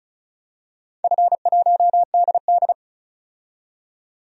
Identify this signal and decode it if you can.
Morse code, heard as a keyed tone, about 36 wpm, 700 Hz. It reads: F1DD
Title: DXCC-100 Call Signs - 4 Characters - 35wpm